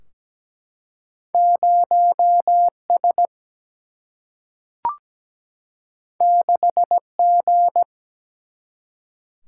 Morse code, heard as a keyed tone, about 17 wpm, 700 Hz. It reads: 0 S 6 G
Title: Single-Letter Single-Number 17wpm